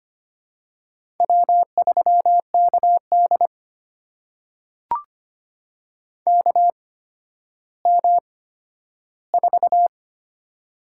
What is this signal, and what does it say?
Morse code, heard as a keyed tone, about 25 wpm, spaced as for 12 wpm, 700 Hz. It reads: W3KD KM4
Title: USA Call Signs - 8x Character Spacing - 25wpm